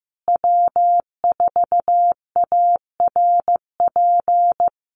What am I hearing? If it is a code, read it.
Morse code, heard as a keyed tone, about 15 wpm, 700 Hz. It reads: W4ARP
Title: USA Call Signs Repeated 3x 15wpm